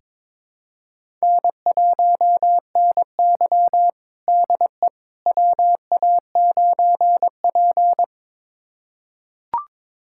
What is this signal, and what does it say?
Morse code, heard as a keyed tone, about 22 wpm, 700 Hz. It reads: N1NY DE WA9P
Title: USA Callsign DE Callsign - 4 Characters - 22wpm